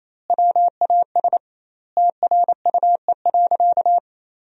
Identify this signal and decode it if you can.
Morse code, heard as a keyed tone, about 28 wpm, 700 Hz. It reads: WAS TRUE.
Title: 2x Word-Spacing - Sentences from Top 400 Words - 28wpm